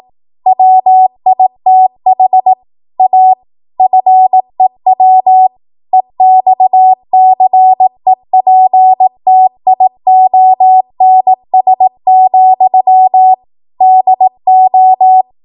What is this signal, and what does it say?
Morse code, heard as a keyed tone, about 18 wpm, 750 Hz. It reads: WITH A FEW EXCEPTIONS, DO
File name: Morse / 18WPM